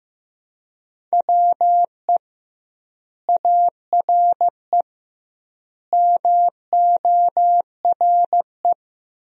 Morse code, heard as a keyed tone, about 15 wpm, 700 Hz. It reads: WE ARE MORE
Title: 2x Word-Spacing - Sentences from Top 500 Words - 15wpm